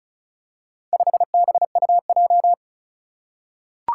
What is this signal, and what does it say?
Morse code, heard as a keyed tone, about 36 wpm, 700 Hz. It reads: 5BUJ
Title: Four Challenging Letter-Number Combinations 35wpm